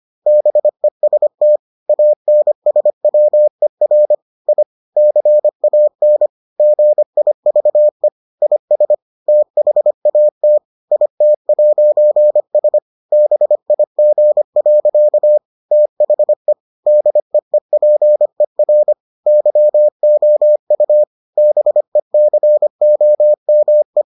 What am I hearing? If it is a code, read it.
Morse code, heard as a keyed tone, about 25 wpm, 600 Hz. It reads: BEST ANSWER I CAN GIVE IS THAT IT'S BIG. THE DEEPER YOU BECOME